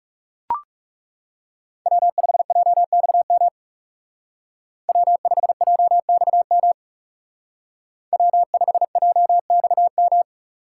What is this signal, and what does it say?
Morse code, heard as a keyed tone, about 35 wpm, 700 Hz. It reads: W5JXM W5JXM W5JXM
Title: Speed Racing - USA 1x3 Call Signs 30wpm